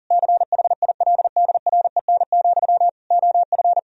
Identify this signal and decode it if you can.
Morse code, heard as a keyed tone, about 37 wpm, 700 Hz. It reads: CHILDREN, OF
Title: Sentences from Top 1000 Words - Review Part 1 - 40wpm